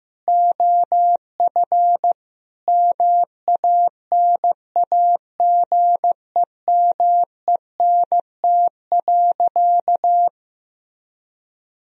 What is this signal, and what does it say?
Morse code, heard as a keyed tone, about 16 wpm, 700 Hz. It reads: OF MANAGEMENT.
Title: Sentences from Top 1000 Words - Review Part 4 - 15wpm